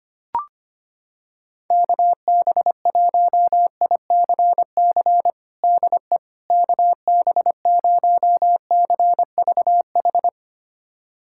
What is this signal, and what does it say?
Morse code, heard as a keyed tone, about 25 wpm, 700 Hz. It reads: KB1ICC DE KB0CVH
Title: USA Callsign DE Callsign 25wpm